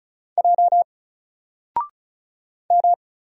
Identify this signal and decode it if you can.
Morse code, heard as a keyed tone, about 36 wpm, 700 Hz. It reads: J M
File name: Single-Letter-Number-35wpm